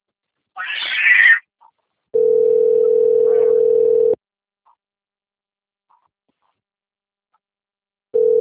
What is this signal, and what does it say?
Ringback tone (North America), 440 + 480 Hz.